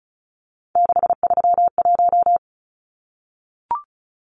Morse code, heard as a keyed tone, about 34 wpm, 700 Hz. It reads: 631
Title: Numbers - 3-Digit 35wpm